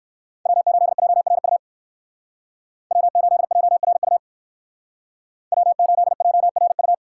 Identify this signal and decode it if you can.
Morse code, heard as a keyed tone, about 58 wpm, 700 Hz. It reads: W8JRU W8JRU W8JRU